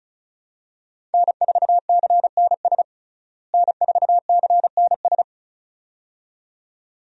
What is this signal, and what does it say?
Morse code, heard as a keyed tone, about 36 wpm, 700 Hz. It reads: N4CNS N4CNS
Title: USA Call Signs - 5 Characters - Repeated 2x - 35wpm